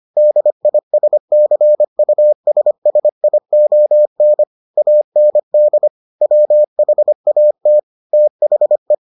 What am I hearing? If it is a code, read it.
Morse code, heard as a keyed tone, about 25 wpm, 600 Hz. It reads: DISCUSSION AND WHAT THE